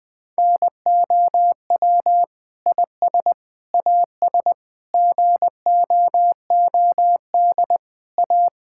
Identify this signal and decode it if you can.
Morse code, heard as a keyed tone, about 21 wpm, 700 Hz. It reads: NOW IS AS GOOD A
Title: Sentences with 8-Word Phrases - 20wpm